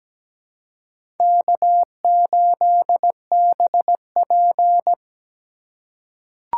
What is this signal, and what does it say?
Morse code, heard as a keyed tone, about 17 wpm, 700 Hz. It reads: K8BP
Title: USA Call Signs - 4 Characters - 2x Character Spacing - 17wpm